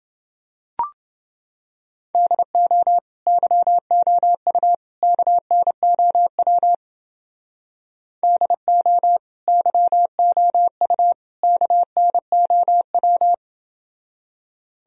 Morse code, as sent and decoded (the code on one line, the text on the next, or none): -.. --- / -.-- --- ..- / -.- -. --- .-- / -.. --- / -.-- --- ..- / -.- -. --- .--
DO YOU KNOW DO YOU KNOW